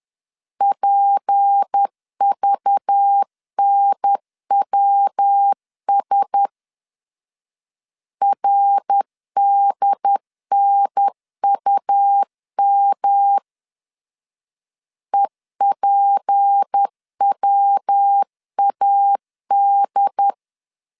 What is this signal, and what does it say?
Morse code, heard as a keyed tone, about 10 wpm, 800 Hz. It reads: PVNWS RDNUM EPWAD
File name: LEZIONI / 6 PKX0 / 6 P 10wpm 2432